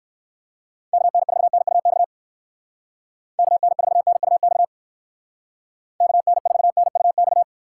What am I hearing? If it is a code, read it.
Morse code, heard as a keyed tone, about 64 wpm, 700 Hz. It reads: KN4NUX KN4NUX KN4NUX